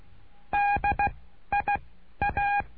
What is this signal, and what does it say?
Morse code, heard as a keyed tone, about 16 wpm, 810 Hz. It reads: D I A
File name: cursos / CURSO DE TELEGRAFIA / Telegrafia - Trilha 09